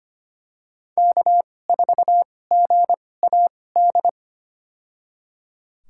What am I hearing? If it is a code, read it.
Morse code, heard as a keyed tone, about 25 wpm, spaced as for 18 wpm, 700 Hz. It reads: K4GAD
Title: USA Call Signs - 2x Character Spacing - 25wpm